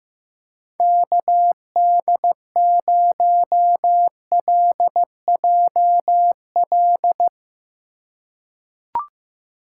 Morse code, heard as a keyed tone, about 15 wpm, 700 Hz. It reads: KD0LJL
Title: USA Call Signs - 4x Character Spacing - 15wpm